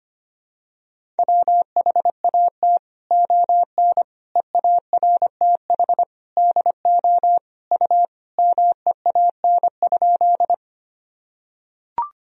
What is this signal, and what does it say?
Morse code, heard as a keyed tone, about 25 wpm, 700 Hz. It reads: WHAT ON EARTH DO U MEAN?
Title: Sentences from Top 200 Words and Abbreviations 25wpm